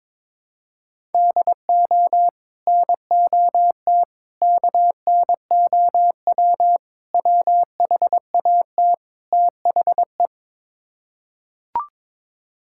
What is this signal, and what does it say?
Morse code, heard as a keyed tone, about 22 wpm, 700 Hz. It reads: DO NOT KNOW WHAT THE
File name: practice / 22wpm / Build A Phrase - 5-Word Phrases - 22wpm